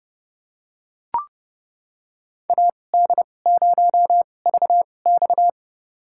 Morse code, heard as a keyed tone, about 32 wpm, spaced as for 22 wpm, 700 Hz. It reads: AD0VX
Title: USA Call Signs - 2x Character Spacing - 30wpm